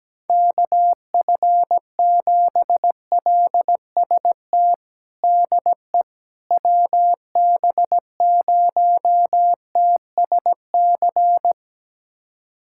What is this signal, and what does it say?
Morse code, heard as a keyed tone, about 17 wpm, 700 Hz. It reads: KF7LST DE WB0TSC
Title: USA Callsign DE Callsign 17wpm